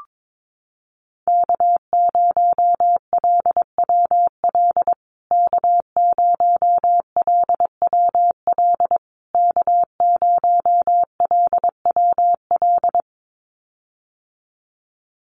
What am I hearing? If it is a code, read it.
Morse code, heard as a keyed tone, about 22 wpm, 700 Hz. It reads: K0LWL K0LWL K0LWL
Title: USA Call Signs Repeated 3x 22wpm